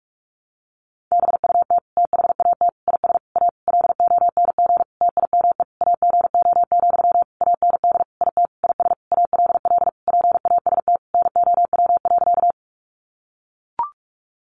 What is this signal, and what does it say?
Morse code, heard as a keyed tone, about 44 wpm, 700 Hz. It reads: BUT THAT IS A LONG TIME AGO, AND IT IS ALL PAST NOW.
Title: Sentences from Top 600 Words 45wpm